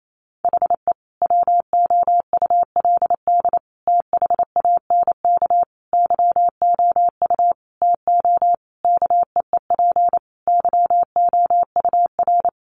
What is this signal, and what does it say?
Morse code, heard as a keyed tone, about 28 wpm, 700 Hz. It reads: HE WOULD THANK YOU TO KEEP YOUR
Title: Sentences from Top 1100 Words 28wpm